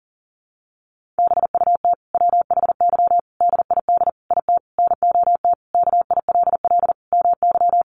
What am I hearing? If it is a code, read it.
Morse code, heard as a keyed tone, about 40 wpm, 700 Hz. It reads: BUT WHY DID IT NOT KILL MY
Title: Sentences from Top 1400 Words 40wpm